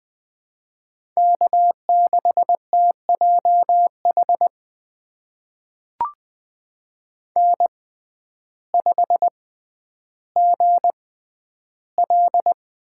Morse code, heard as a keyed tone, about 20 wpm, spaced as for 11 wpm, 700 Hz. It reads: K6TJH N5GL
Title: USA Call Signs - 5 Characters - 6x Character Spacing - 20wpm